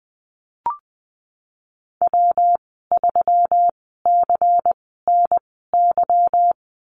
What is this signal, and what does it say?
Morse code, heard as a keyed tone, about 19 wpm, spaced as for 14 wpm, 700 Hz. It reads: W3CNY
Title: 147 Lesson 24g - Letter y - Call Signs 20wpm